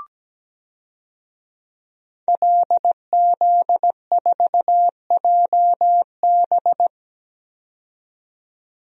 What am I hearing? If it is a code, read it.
Morse code, heard as a keyed tone, about 17 wpm, 700 Hz. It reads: LZ4JB
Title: DXCC-100 Call Signs - 5 Characters - 17wpm